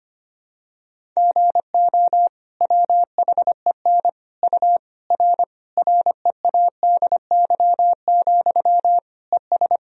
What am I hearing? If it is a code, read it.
Morse code, heard as a keyed tone, about 25 wpm, 700 Hz. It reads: GO WHEN U R READY, ES
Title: Sentences from Top 500 Words and Abbreviations 25wpm